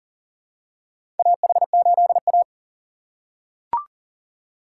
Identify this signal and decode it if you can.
Morse code, heard as a keyed tone, about 40 wpm, 700 Hz. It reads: AH8A
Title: USA Call Signs - 4 Characters - 3x Character Spacing - 40wpm